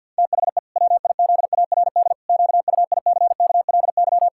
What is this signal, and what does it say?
Morse code, heard as a keyed tone, about 52 wpm, 700 Hz. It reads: THE WIZARD QUICKLY
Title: Pangram Challenge v2 - 50wpm